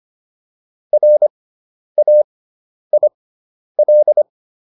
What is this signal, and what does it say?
Morse code, heard as a keyed tone, about 25 wpm, spaced as for 10 wpm, 600 Hz. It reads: RAIL